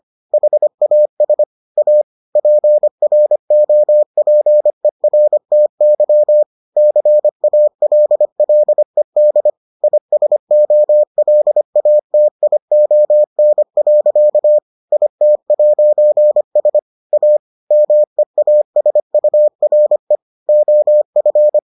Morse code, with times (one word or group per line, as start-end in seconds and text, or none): 0.3-1.4 HAS
1.8-2.0 A
2.4-6.4 PROPERTY
6.8-9.5 CALLED
9.8-14.6 ISOLATION.
14.9-16.8 IT'S
17.1-17.4 A
17.7-20.2 MEASURE
20.5-21.6 OF